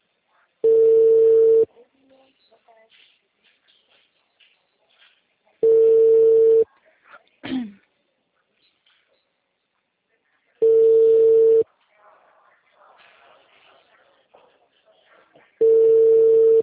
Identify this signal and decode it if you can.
Ringback tone (China, 450 Hz).